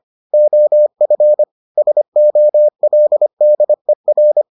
Morse code, heard as a keyed tone, about 25 wpm, 600 Hz. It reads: OF SOLDER